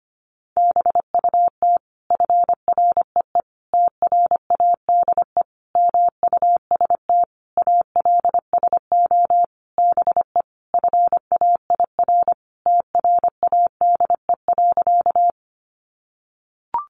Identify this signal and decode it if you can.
Morse code, heard as a keyed tone, about 25 wpm, 700 Hz. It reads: BUT FREE TRADE MUST ALSO BE FAIR TRADE.
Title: Sentences from Top 1300 Words 25wpm